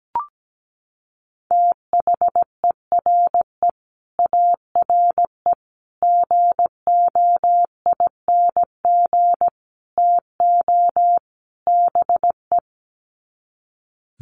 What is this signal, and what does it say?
Morse code, heard as a keyed tone, about 17 wpm, 700 Hz. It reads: THERE ARE GOING TO BE
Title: Sets of 5 Words 17wpm